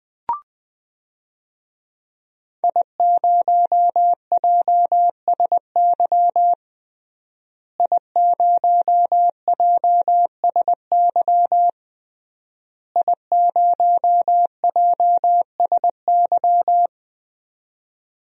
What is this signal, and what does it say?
Morse code, heard as a keyed tone, about 20 wpm, 700 Hz. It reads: I0JSY I0JSY I0JSY
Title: DXCC-100 Call Signs - 5 Characters - Repeated 3x - 20wpm